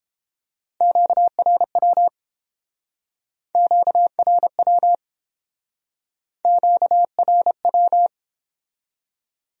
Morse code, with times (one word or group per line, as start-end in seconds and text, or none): 0.8-2.1 QRW
3.5-5.0 QRW
6.5-8.1 QRW